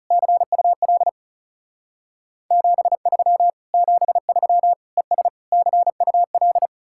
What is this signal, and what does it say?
Morse code, heard as a keyed tone, about 37 wpm, 700 Hz. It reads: CUL 73 73 ES CUL
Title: Speed-Racing Common QSO Phrases - 30wpm